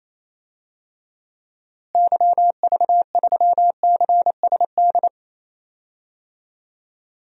Morse code, heard as a keyed tone, about 28 wpm, 700 Hz. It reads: YV3CSD
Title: DXCC-100 Call Signs - 6 Characters - Repeated 2x - 28wpm